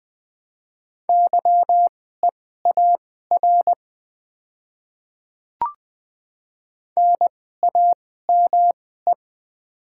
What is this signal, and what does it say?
Morse code, heard as a keyed tone, about 20 wpm, spaced as for 14 wpm, 700 Hz. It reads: YEAR NAME
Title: 2x Character Spacing - Top 200 Words - 20wpm